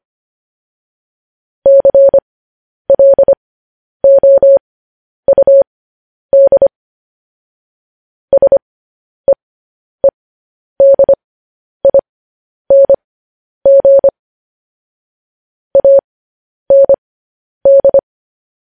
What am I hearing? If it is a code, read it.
Morse code, heard as a keyed tone, about 25 wpm, spaced as for 10 wpm, 550 Hz. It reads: CLOUD SEEDING AND
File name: podcast / News headlines-1x-25x10